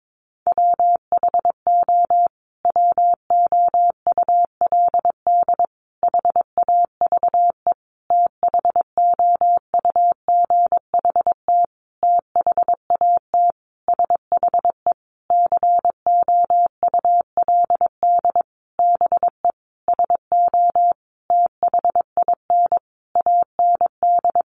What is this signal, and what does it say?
Morse code, heard as a keyed tone, about 22 wpm, 700 Hz. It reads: WHO WOULD HAVE THOUGHT THAT SHE COULD BE SO THIN AND
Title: Sentences from Top 1200 Words 22wpm